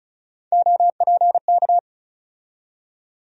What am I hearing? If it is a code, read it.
Morse code, heard as a keyed tone, about 34 wpm, 700 Hz. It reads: OPK
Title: USA Call Sign Postfixes - 3 Letters 35wpm